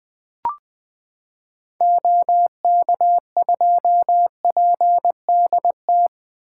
Morse code, heard as a keyed tone, about 20 wpm, 700 Hz. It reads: OK2PDT
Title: 2023 SOTA Chaser Call Signs - 20wpm